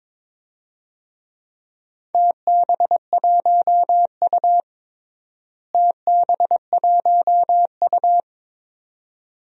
Morse code, heard as a keyed tone, about 21 wpm, 700 Hz. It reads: TB1U TB1U
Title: DXCC-100 Call Signs - 4 Characters - Repeated 2x - 22wpm